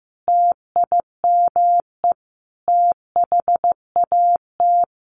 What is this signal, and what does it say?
Morse code, heard as a keyed tone, about 15 wpm, 700 Hz. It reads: TIME THAT
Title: Sets of 5 Words Encore 15wpm